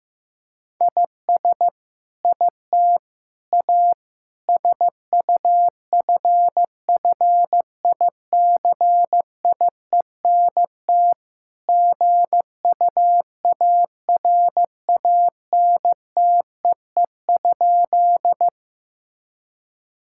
Morse code, heard as a keyed tone, about 15 wpm, 700 Hz. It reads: IS IT A SUFFICIENT GUARANTEE?
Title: Sentences from Top 1600 Words 15wpm